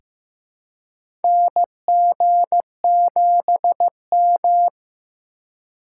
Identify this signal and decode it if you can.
Morse code, heard as a keyed tone, about 15 wpm, 690 Hz. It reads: NG7M
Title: USA Call Signs - 4 Characters - 2x Character Spacing - 15wpm